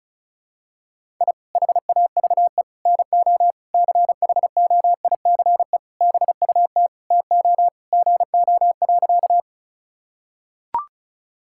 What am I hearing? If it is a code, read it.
Morse code, heard as a keyed tone, about 36 wpm, 700 Hz. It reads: I HAVE NO CHOICE BUT TO GO.
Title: Sentences from Top 1000 Words - Review Part 6 - 35wpm